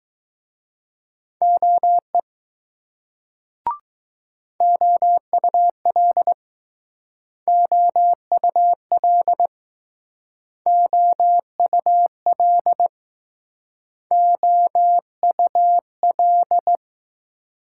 Morse code, heard as a keyed tone, about 20 wpm, 700 Hz. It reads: OE OUL OUL OUL OUL